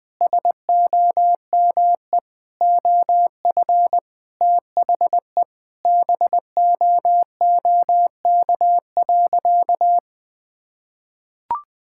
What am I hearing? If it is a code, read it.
Morse code, heard as a keyed tone, about 20 wpm, 700 Hz. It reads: SOME OF THE BOOK.
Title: Sentences with 2-Word Phrases - 20wpm